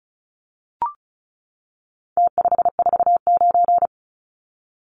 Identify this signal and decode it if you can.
Morse code, heard as a keyed tone, about 34 wpm, 700 Hz. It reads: T549
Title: Numbers - Contest Style - 4-Digit 35wpm